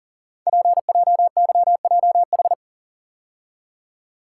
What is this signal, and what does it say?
Morse code, heard as a keyed tone, about 42 wpm, 700 Hz. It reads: PJYJH
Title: Code Groups - 5 Letters - 40wpm